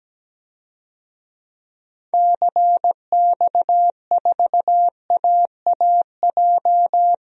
Morse code, heard as a keyed tone, about 17 wpm, 700 Hz. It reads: CX4AAJ
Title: DXCC-100 Call Signs - 6 Characters - Repeated 3x - 17wpm